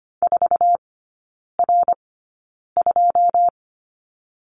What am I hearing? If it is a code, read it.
Morse code, heard as a keyed tone, about 25 wpm, spaced as for 9 wpm, 690 Hz. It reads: 4R2
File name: Lesson 15e - Number 5 - Three Characters 25wpm